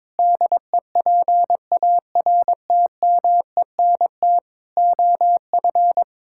Morse code, heard as a keyed tone, about 21 wpm, 700 Hz. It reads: DEPARTMENT OF
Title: Sentences from Top 1000 Words - Review Part 5 - 22wpm